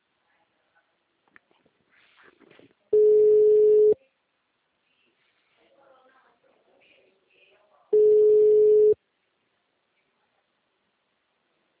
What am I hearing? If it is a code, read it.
Ringback tone (Europe, 425 Hz), with speech.